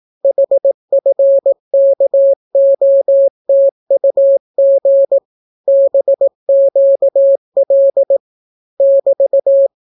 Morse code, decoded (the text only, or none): HFKOTUG BQL <BT>